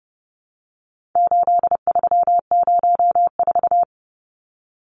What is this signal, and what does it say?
Morse code, heard as a keyed tone, about 30 wpm, 700 Hz. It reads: 8304
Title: Numbers - 4-Digit 30wpm